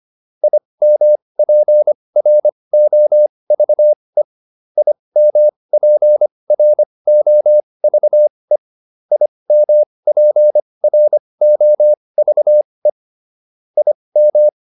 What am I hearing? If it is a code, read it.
Morse code, heard as a keyed tone, about 25 wpm, spaced as for 20 wpm, 600 Hz. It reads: IMPROVE IMPROVE IMPROVE IM